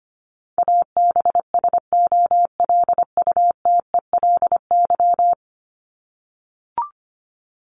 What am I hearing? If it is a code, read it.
Morse code, heard as a keyed tone, about 25 wpm, 700 Hz. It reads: ABSOLUTELY
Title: LY Suffix - Top 100 Words with Base Words 25wpm